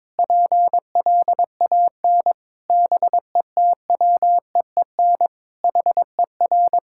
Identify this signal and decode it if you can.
Morse code, heard as a keyed tone, about 22 wpm, 700 Hz. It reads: PLAN BETWEEN HER
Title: Sentences from Top 1000 Words - Review Part 6 - 22wpm